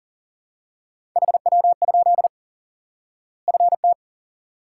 Morse code, heard as a keyed tone, about 40 wpm, 700 Hz. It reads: HW? FT
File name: IQE Course 3 - Lesson 89 - YAGI - Two Elements - 40wpm